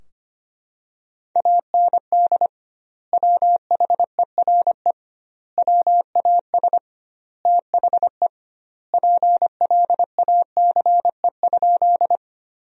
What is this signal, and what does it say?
Morse code, heard as a keyed tone, about 25 wpm, 700 Hz. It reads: AND WHERE WAS THE PLACE?
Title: 2x Word-Spacing - Sentences from Top 200 Words - 25wpm